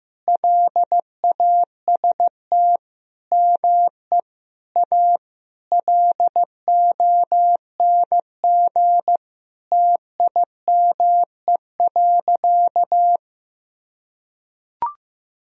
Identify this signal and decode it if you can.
Morse code, heard as a keyed tone, about 15 wpm, 700 Hz. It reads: LAST ME A LONG TIME.
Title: Sentences from Top 1000 Words - Review Part 2 - 15wpm